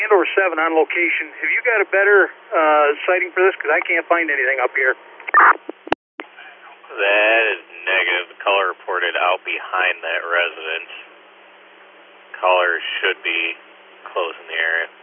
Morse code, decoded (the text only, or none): none